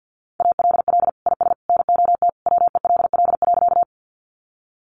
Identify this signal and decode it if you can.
Morse code, heard as a keyed tone, about 50 wpm, 700 Hz. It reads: ALL IS NOT WELL.
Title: Sentences from Top 600 Words 50wpm